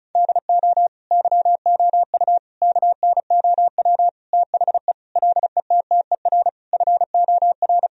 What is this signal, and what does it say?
Morse code, heard as a keyed tone, about 34 wpm, 700 Hz. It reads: DO YOU KNOW THE LETTER FOR
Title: Sentences from Top 1000 Words - Review Part 2 - 35wpm